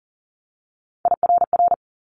Morse code, heard as a keyed tone, about 39 wpm, 700 Hz. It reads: IRR
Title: Multi-letter Phonemes 40wpm